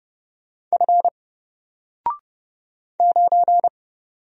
Morse code, heard as a keyed tone, about 30 wpm, 700 Hz. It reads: F 9
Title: Single Character (Letter:Number:Punctuation:Prosign) 30wpm